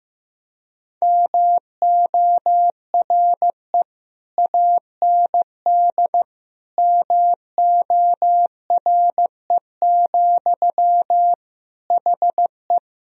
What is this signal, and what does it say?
Morse code, heard as a keyed tone, about 15 wpm, 700 Hz. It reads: MORE AND MORE, HE
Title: Sentences from Top 1000 Words - Review Part 3 - 15wpm